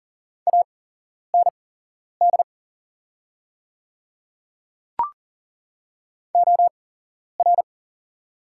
Morse code, heard as a keyed tone, about 40 wpm, spaced as for 11 wpm, 700 Hz. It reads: AND OR
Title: 8x Character Spacing - Top 100 Words - 40wpm